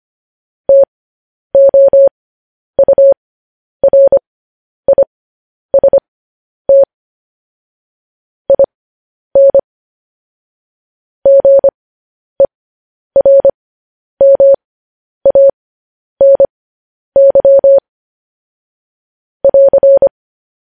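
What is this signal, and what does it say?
Morse code, heard as a keyed tone, about 25 wpm, spaced as for 10 wpm, 550 Hz. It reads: TOURIST IN GERMANY <AR>
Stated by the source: Morse code news headlines practice transmission.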